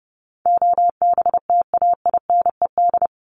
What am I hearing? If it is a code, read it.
Morse code, heard as a keyed tone, about 28 wpm, 700 Hz. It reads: OBTAINED
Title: ED Suffix - Top 100 Words with Base Words 30wpm